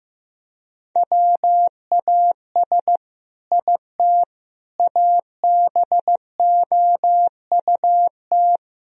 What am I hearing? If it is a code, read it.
Morse code, heard as a keyed tone, about 15 wpm, 700 Hz. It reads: WAS IT ABOUT
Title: Sentences from Top 1000 Words - Review Part 3 - 15wpm